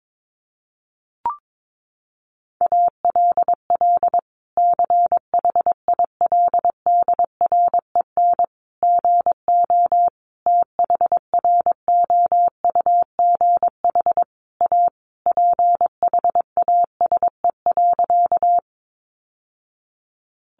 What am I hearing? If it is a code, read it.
Morse code, heard as a keyed tone, about 22 wpm, 700 Hz. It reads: ALL CHILDREN GO THROUGH A PHASE.
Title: Sentences from Top 1200 Words 22wpm